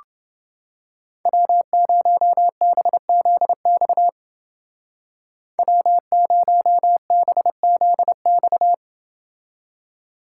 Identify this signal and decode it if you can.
Morse code, heard as a keyed tone, about 28 wpm, 700 Hz. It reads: W0BZX W0BZX